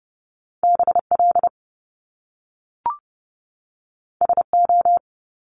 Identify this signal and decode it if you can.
Morse code, heard as a keyed tone, about 30 wpm, 700 Hz. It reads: BL SO
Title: Top Two Letter Combinations in Words 30wpm